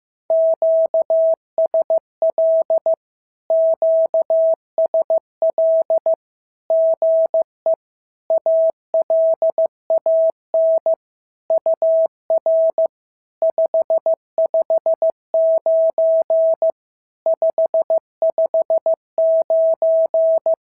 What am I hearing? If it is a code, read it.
Morse code, heard as a keyed tone, about 15 wpm, 650 Hz. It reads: QSL QSL GE ALAN UR 559 559
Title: SOTA QSO Practice 15wpm